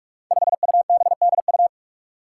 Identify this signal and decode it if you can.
Morse code, heard as a keyed tone, about 44 wpm, 700 Hz. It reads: 5UBDU
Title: Five Challenging Letter-Number Combinations 45wpm